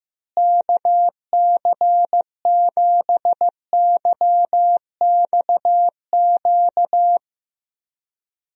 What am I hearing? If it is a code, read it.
Morse code, heard as a keyed tone, about 15 wpm, 700 Hz. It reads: KC7YXQ
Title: USA Call Signs - 6 Characters - Repeated 3x - 15wpm